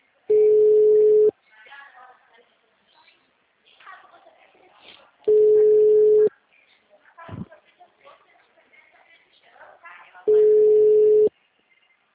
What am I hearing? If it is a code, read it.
Ringback tone (Europe, 425 Hz), with speech.